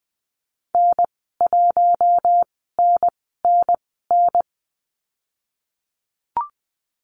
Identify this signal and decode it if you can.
Morse code, heard as a keyed tone, about 20 wpm, spaced as for 14 wpm, 700 Hz. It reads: N1NNN
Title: 069 Lesson 12g - Letter d - Call Signs 20wpm